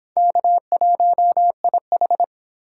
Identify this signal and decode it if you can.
Morse code, heard as a keyed tone, about 23 wpm, 700 Hz. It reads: K1IH